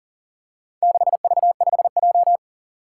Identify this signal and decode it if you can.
Morse code, heard as a keyed tone, about 37 wpm, 700 Hz. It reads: 6V5J
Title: Four Challenging Letter-Number Combinations 40wpm